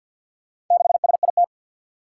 Morse code, heard as a keyed tone, about 47 wpm, 700 Hz. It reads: 6SIT
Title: Four Challenging Letter-Number Combinations 50wpm